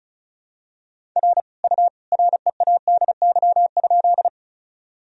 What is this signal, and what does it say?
Morse code, heard as a keyed tone, about 34 wpm, 700 Hz. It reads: R U READY?
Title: Sentences from Top 400 Words and Abbreviations 35wpm